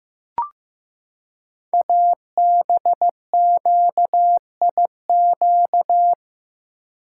Morse code, heard as a keyed tone, about 15 wpm, 700 Hz. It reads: ABQIQ
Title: Code Groups - 5 Letters - 15wpm